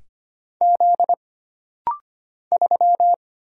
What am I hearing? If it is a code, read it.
Morse code, heard as a keyed tone, about 25 wpm, 700 Hz. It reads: Z 3
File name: Single Character (Letter:Number:Punctuation:Prosign) - Rapid-Fire - 25wpm